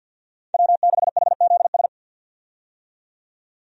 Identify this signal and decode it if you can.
Morse code, heard as a keyed tone, about 52 wpm, 700 Hz. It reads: W6HZS
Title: USA Call Signs - 5 Characters - 50wpm